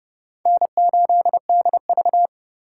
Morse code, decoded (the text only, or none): N8DV